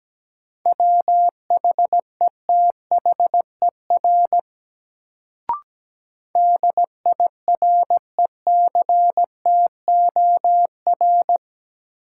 Morse code, heard as a keyed tone, about 17 wpm, 700 Hz. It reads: WHETHER DIRECTOR
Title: Sets of 3 Words - New Word Drill - 17wpm